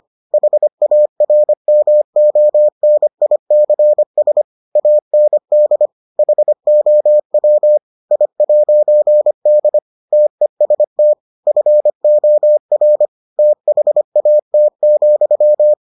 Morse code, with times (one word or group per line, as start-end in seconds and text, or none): 0.3-4.4 HARMONICS
4.8-5.9 AND
6.2-7.8 HOW
8.1-9.8 I'D
10.1-11.1 TEST
11.5-13.1 FOR
13.4-15.7 THAT,